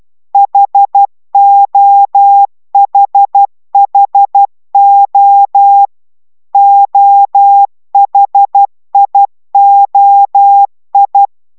Morse code, heard as a keyed tone, about 12 wpm, 800 Hz. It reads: HOHHO OHIOI